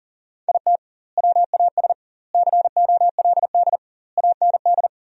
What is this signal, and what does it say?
Morse code, heard as a keyed tone, about 38 wpm, 700 Hz. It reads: IT WAS COLD AND
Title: 2x Word-Spacing - Sentences from Top 500 Words - 40wpm